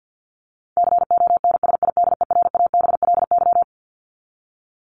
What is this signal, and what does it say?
Morse code, heard as a keyed tone, about 49 wpm, 700 Hz. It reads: CONSIDERABLY
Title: LY Suffix - Top 100 Words with Base Words 50wpm